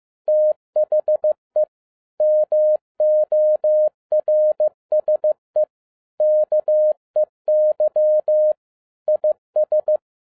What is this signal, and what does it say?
Morse code, heard as a keyed tone, about 15 wpm, 610 Hz. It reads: THE MORSE KEY IS